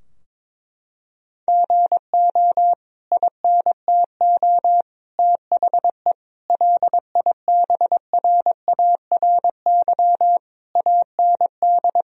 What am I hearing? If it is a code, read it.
Morse code, heard as a keyed tone, about 22 wpm, 700 Hz. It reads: GO INTO THE LIBRARY AND
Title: Sentences from Top 1000 Words - Review Part 4 - 22wpm